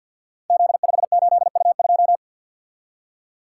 Morse code, heard as a keyed tone, about 52 wpm, 700 Hz. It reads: Z58U2
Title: Code Groups - 5 Letter:Numbers - 50wpm